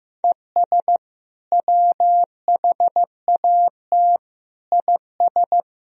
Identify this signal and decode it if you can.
Morse code, heard as a keyed tone, about 15 wpm, 700 Hz. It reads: ES WHAT IS
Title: Sentences from Top 100 Words and Abbreviations 15wpm